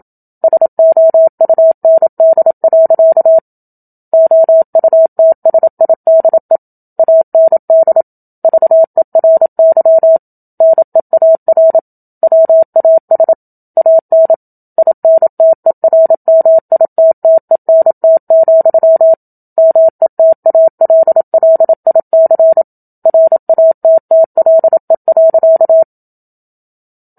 Morse code, heard as a keyed tone, about 27 wpm, 650 Hz. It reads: SOUND. OUTSIDE AND VERY NEAR WAS AN INTERMITTENT, METALLIC RATTLE.
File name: Book Two / 01-Chapter 1 - 27 WPM